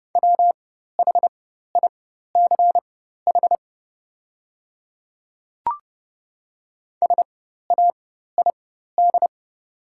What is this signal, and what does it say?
Morse code, heard as a keyed tone, about 30 wpm, spaced as for 14 wpm, 700 Hz. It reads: WHICH SAID
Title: 4x Character Spacing - Top 100 Words - 30wpm